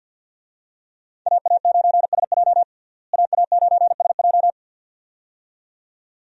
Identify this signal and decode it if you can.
Morse code, heard as a keyed tone, about 52 wpm, 700 Hz. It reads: AA9SJ AA9SJ
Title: USA Call Signs - 5 Characters - Repeated 2x - 50wpm